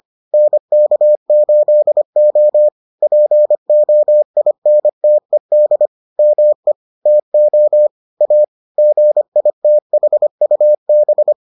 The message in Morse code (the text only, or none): NK8O POINTED ME TO A GITHUB